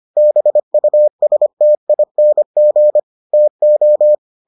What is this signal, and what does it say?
Morse code, heard as a keyed tone, about 25 wpm, 600 Hz. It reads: BUSTING TO